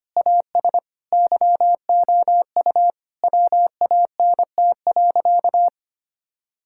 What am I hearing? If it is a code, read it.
Morse code, heard as a keyed tone, about 25 wpm, 700 Hz. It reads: AS YOU WANT.
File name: practice / 25wpm / Sentences with 3-Word Phrases - 25wpm